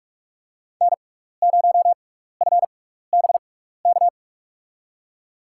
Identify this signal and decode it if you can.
Morse code, heard as a keyed tone, about 46 wpm, spaced as for 16 wpm, 690 Hz. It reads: N0FBK
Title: USA Call Signs - 5 Characters - 6x Character Spacing - 45wpm